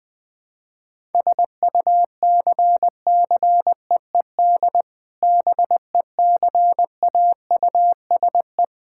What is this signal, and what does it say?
Morse code, heard as a keyed tone, about 20 wpm, 700 Hz. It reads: SUCCEED BECAUSE
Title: Sentences with 2-Word Phrases - 20wpm